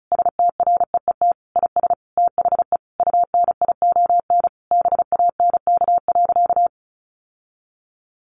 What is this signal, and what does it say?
Morse code, heard as a keyed tone, about 34 wpm, 700 Hz. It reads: STREET IS THE UNION BANK.
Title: Sentences from Top 1400 Words 35wpm